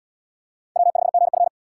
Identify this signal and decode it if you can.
Morse code, heard as a keyed tone, about 69 wpm, 700 Hz. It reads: W5GF